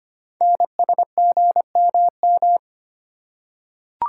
Morse code, heard as a keyed tone, about 25 wpm, 700 Hz. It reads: NSGMM
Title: Code Groups - 5 Letters - 25wpm